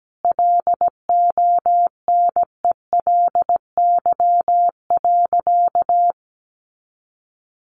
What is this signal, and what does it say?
Morse code, heard as a keyed tone, about 17 wpm, 700 Hz. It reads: LONELY.
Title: Additional Easy Phrases 17wpm